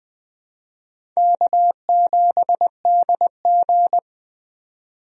Morse code, heard as a keyed tone, about 20 wpm, 700 Hz. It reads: K7DG
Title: USA Call Signs - 4 Characters - 20wpm